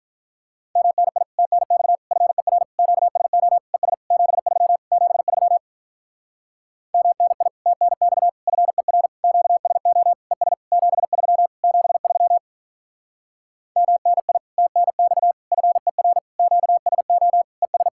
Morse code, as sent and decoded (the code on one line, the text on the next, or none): -- -. .. / - -. -..- / ..-. .-. / --.- ... --- / . ... / --... ...-- / --... ...-- / -- -. .. / - -. -..- / ..-. .-. / --.- ... --- / . ... / --... ...-- / --... ...-- / -- -. .. / - -. -..- / ..-. .-. / --.- ... --- / . ...
MNI TNX FER QSO ES 73 73 MNI TNX FER QSO ES 73 73 MNI TNX FER QSO ES